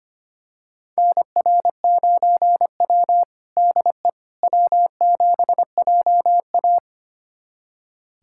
Morse code, heard as a keyed tone, about 25 wpm, 700 Hz. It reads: NR9W DE W7JA
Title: USA Callsign DE Callsign - 4 Characters - 25wpm